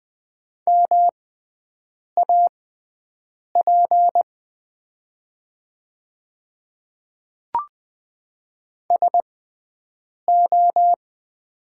Morse code, heard as a keyed tone, about 21 wpm, spaced as for 7 wpm, 700 Hz. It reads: MAP SO
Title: 6x Character Spacing - Top 500 Words - 20wpm